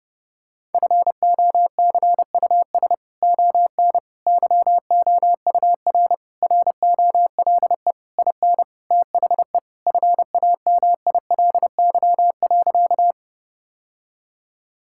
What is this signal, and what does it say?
Morse code, heard as a keyed tone, about 30 wpm, 700 Hz. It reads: FOCUS ON YOUR ROLE IN THE FAMILY.
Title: Sentences from Top 1000 Words - Review Part 6 - 30wpm